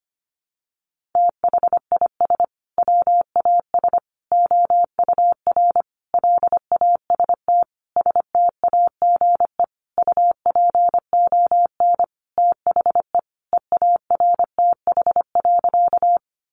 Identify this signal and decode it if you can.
Morse code, heard as a keyed tone, about 25 wpm, 700 Hz. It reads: THIS WAS OUR LAST STAGE UPON THE EARTH.
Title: Sentences from Top 1200 Words 25wpm